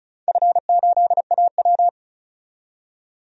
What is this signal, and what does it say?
Morse code, heard as a keyed tone, about 36 wpm, 700 Hz. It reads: F8AW
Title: DXCC-100 Call Signs - 4 Characters - 35wpm